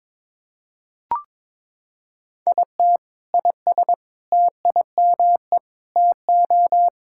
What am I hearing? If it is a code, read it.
Morse code, heard as a keyed tone, about 22 wpm, 700 Hz. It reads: IT IS TIME TO
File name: practice / 22wpm / Sentences with 4-Word Phrases - 22wpm